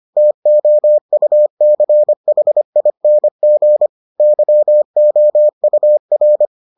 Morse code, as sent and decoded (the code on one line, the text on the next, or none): - --- ..- -.-. .... .. -. --. / -.-- --- ..- .-.
TOUCHING YOUR